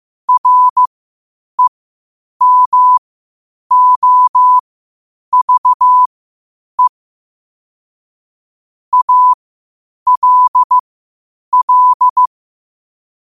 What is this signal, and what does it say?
Morse code, heard as a keyed tone, about 15 wpm, spaced as for 8 wpm, 1000 Hz. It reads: REMOVE ALL
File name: qotdfiles / 8wpm